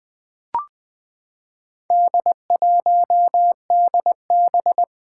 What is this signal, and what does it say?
Morse code, heard as a keyed tone, about 20 wpm, 700 Hz. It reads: D1DB
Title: Four Letter-Number Combinations - 20wpm